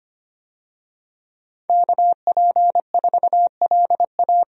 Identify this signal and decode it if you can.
Morse code, heard as a keyed tone, about 25 wpm, 700 Hz. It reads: KP4LA
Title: USA Call Signs - 5 Characters - Repeated 2x - 25wpm